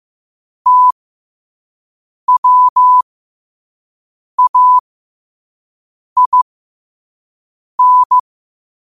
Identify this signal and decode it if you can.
Morse code, heard as a keyed tone, about 15 wpm, spaced as for 5 wpm, 1000 Hz. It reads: TWAIN
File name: qotdfiles / 5wpm